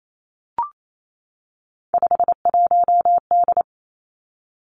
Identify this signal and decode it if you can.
Morse code, heard as a keyed tone, about 28 wpm, 700 Hz. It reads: 51D
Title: Three Challenging Letter-Number Combinations 28wpm